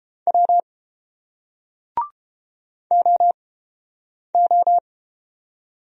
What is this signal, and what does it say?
Morse code, heard as a keyed tone, about 31 wpm, spaced as for 8 wpm, 700 Hz. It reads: W OO